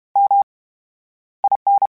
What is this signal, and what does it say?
Morse code, heard as a keyed tone, about 32 wpm, 800 Hz. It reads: M I N
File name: Special INGs / Special INGs-mp3s / VerySlowToFaster Ing @30